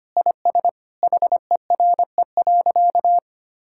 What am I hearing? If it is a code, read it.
Morse code, heard as a keyed tone, about 25 wpm, 700 Hz. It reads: IS HERE.
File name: practice / 25wpm / Sentences with 3-Word Phrases - 25wpm